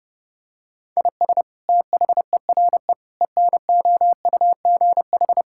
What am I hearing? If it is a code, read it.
Morse code, heard as a keyed tone, about 32 wpm, 700 Hz. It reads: IS THERE ENOUGH
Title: Sentences from Top 1000 Words - Review Part 5 - 30wpm